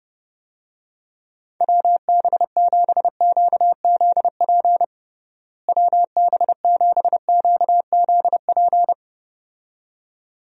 Morse code, heard as a keyed tone, about 30 wpm, 700 Hz. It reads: WB7QZP WB7QZP
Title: USA Call Signs - 6 Characters - Repeated 2x - 30wpm